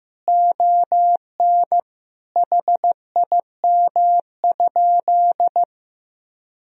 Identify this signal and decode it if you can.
Morse code, heard as a keyed tone, about 16 wpm, 700 Hz. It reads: ON HIM?
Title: Sentences from Top 1000 Words - Review Part 1 - 15wpm